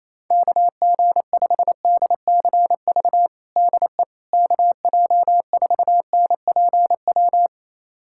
Morse code, heard as a keyed tone, about 28 wpm, 700 Hz. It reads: KG5DCV DE KJ4NPW
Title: USA Callsign DE Callsign - 6 Characters - 28wpm